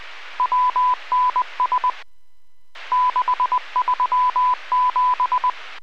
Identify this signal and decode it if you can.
Morse code, heard as a keyed tone, about 20 wpm, 1000 Hz. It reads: WNS 637